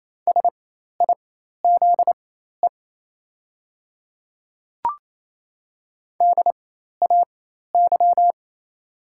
Morse code, heard as a keyed tone, about 28 wpm, spaced as for 13 wpm, 700 Hz. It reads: SIZE DAY
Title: 4x Character Spacing - Top 500 Words - 28wpm